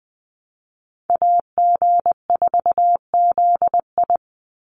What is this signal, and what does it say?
Morse code, heard as a keyed tone, about 20 wpm, 700 Hz. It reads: AG4ZI
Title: USA Call Signs 20wpm